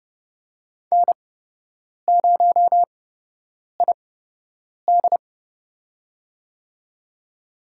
Morse code, heard as a keyed tone, about 30 wpm, spaced as for 8 wpm, 700 Hz. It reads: N0ID
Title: USA Call Signs - 4 Characters - 8x Character Spacing - 30wpm